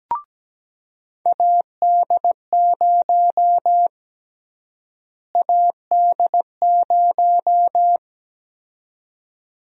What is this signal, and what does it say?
Morse code, heard as a keyed tone, about 17 wpm, 700 Hz. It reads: AD0 AD0
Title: USA Call Sign Prefixes - Repeated 2x - 17wpm